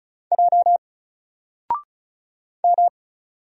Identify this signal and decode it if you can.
Morse code, heard as a keyed tone, about 35 wpm, 700 Hz. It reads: J M
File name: Single-Letter-Number-35wpm